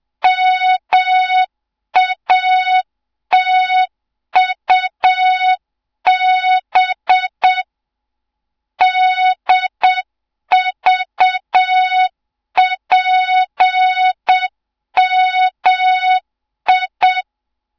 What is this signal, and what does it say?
Morse code, heard as a keyed tone, about 7 wpm, 3700 Hz. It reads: MATUB DVPMI
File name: Morze-HA2MN / hang / 07 GYAKORLAT